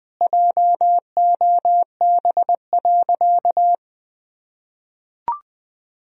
Morse code, heard as a keyed tone, about 19 wpm, 700 Hz. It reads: JOB.
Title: Sentences from Top 1000 Words - Review Part 3 - 20wpm